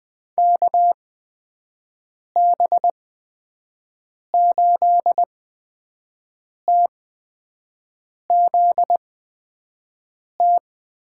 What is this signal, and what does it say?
Morse code, heard as a keyed tone, about 20 wpm, spaced as for 5 wpm, 700 Hz. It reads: KB8TZT
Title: USA Call Signs - 6 Characters - 8x Character Spacing - 20wpm